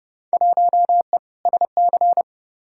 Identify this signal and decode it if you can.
Morse code, heard as a keyed tone, about 31 wpm, 700 Hz. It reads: 1E SC
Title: Field Day Exchanges - 30wpm